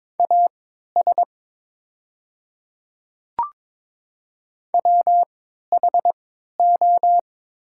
Morse code, heard as a keyed tone, about 22 wpm, spaced as for 13 wpm, 700 Hz. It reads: AS WHO
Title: 3x Character Spacing - Top 200 Words - 22wpm